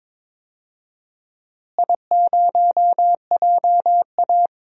Morse code, heard as a keyed tone, about 22 wpm, 700 Hz. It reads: I0JA
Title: DXCC-100 Call Signs - 4 Characters - Repeated 2x - 22wpm